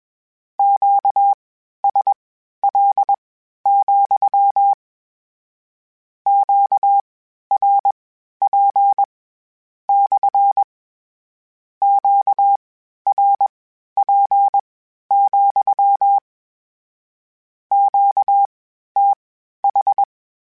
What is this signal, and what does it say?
Morse code, heard as a keyed tone, about 21 wpm, spaced as for 12 wpm, 800 Hz. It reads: QSL, QRP / QRP, QTH
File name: Q Calls / Q Calls-mp3s / Q CALLS SPSL @20